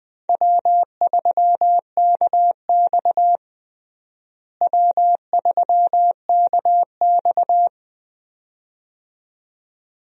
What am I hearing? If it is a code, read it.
Morse code, heard as a keyed tone, about 20 wpm, 700 Hz. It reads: W3KX W3KX
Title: USA Call Signs - 4 Characters - Repeated 2x - 20wpm